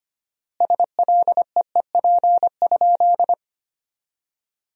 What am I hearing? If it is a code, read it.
Morse code, heard as a keyed tone, about 25 wpm, 700 Hz. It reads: SLEEP?
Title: 4x Word-Spacing - Sentences from Top 500 Words - 25wpm